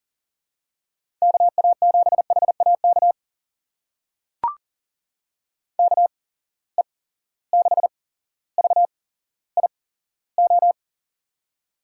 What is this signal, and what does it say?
Morse code, heard as a keyed tone, about 38 wpm, spaced as for 17 wpm, 700 Hz. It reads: KA7HAK KE6VIO
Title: USA Call Signs - 6 Characters - 8x Character Spacing - 40wpm